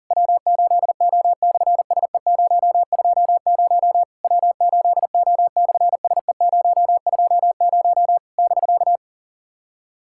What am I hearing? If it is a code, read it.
Morse code, heard as a keyed tone, about 43 wpm, 700 Hz. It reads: W8O/SE020 W8O/SE020 <BK>
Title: SOTA QSO Practice 40wpm